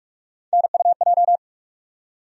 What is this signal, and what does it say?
Morse code, heard as a keyed tone, about 46 wpm, 700 Hz. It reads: NUJ
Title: USA Call Sign Postfixes - 3 Letters 45wpm